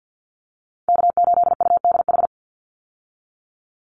Morse code, heard as a keyed tone, about 50 wpm, 700 Hz. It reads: K8UDH
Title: Call Signs Repeated 3x - 50wpm